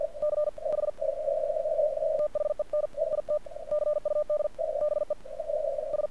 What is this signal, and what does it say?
Morse code, heard as a keyed tone, about 50 wpm, spaced as for 43 wpm, 1200 Hz, with noise. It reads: CH THEN IT CUD BE S
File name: archived / CRSnet